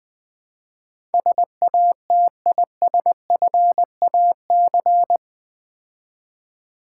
Morse code, heard as a keyed tone, about 20 wpm, 700 Hz. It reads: SATISFAC
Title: Build A Word - 4 Letters At A Time - 12-Letter Words - 20wpm